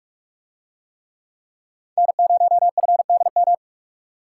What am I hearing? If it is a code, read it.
Morse code, heard as a keyed tone, about 44 wpm, 700 Hz. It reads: N0FDM